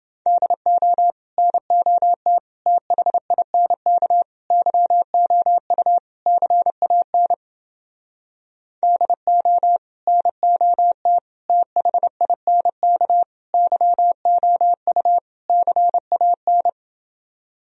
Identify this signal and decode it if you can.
Morse code, heard as a keyed tone, about 28 wpm, 700 Hz. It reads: DO NOT THINK YOU CAN DO NOT THINK YOU CAN